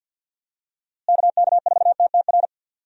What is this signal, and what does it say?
Morse code, heard as a keyed tone, about 50 wpm, 700 Hz. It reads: KK4TTR
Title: USA Call Signs - 6 Characters - Repeated 2x - 50wpm